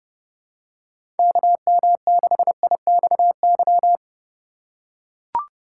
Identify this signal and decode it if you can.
Morse code, heard as a keyed tone, about 30 wpm, 700 Hz. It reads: KM6IXY
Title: USA Call Signs - 6 Characters - 2x Character Spacing - 30wpm